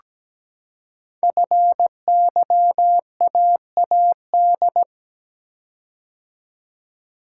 Morse code, heard as a keyed tone, about 17 wpm, 700 Hz. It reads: FYAAD
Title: Code Groups - 5 Letters - 17wpm